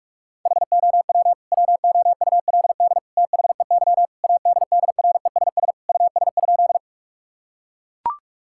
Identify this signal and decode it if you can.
Morse code, heard as a keyed tone, about 42 wpm, 700 Hz. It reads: HOW WOULD THEY ADDRESS US?
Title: Sentences from Top 1000 Words - Review Part 6 - 45wpm